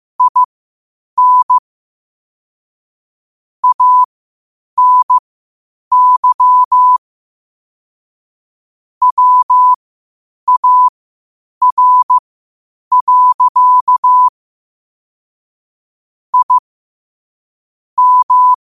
Morse code, heard as a keyed tone, about 15 wpm, spaced as for 8 wpm, 1000 Hz. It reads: IN ANY WAR. I M
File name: qotdfiles / 8wpm